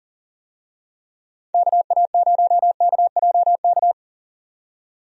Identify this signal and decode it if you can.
Morse code, heard as a keyed tone, about 40 wpm, 700 Hz. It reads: KA0KJK
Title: USA Call Signs - 6 Characters - 40wpm